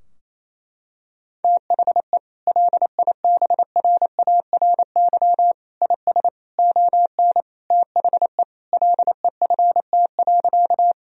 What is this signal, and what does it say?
Morse code, heard as a keyed tone, about 28 wpm, 700 Hz. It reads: THE LIBRARY IS ON THE LEFT.
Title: Sentences from Top 1000 Words - Review Part 4 - 28wpm